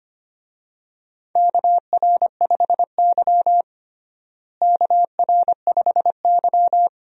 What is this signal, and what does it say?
Morse code, heard as a keyed tone, about 25 wpm, 700 Hz. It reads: KR5Y KR5Y
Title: USA Call Signs - 4 Characters - Repeated 2x - 25wpm